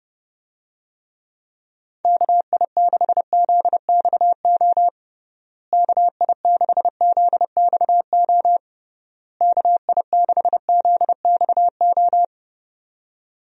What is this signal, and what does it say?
Morse code, heard as a keyed tone, about 32 wpm, 700 Hz. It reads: KI6ZXO KI6ZXO KI6ZXO
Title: USA Call Signs - 6 Characters - Repeated 3x - 30wpm